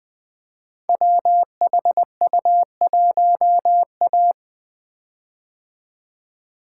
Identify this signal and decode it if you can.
Morse code, heard as a keyed tone, about 20 wpm, 700 Hz. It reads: WHU1A
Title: Code Groups - 5 Letter:Numbers - 20wpm